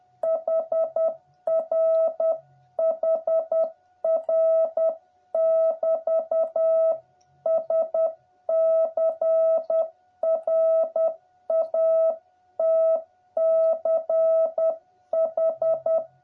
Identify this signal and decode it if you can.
Morse code, heard as a keyed tone, about 10 wpm, 650 Hz. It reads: HR HR <BT> SCRATCH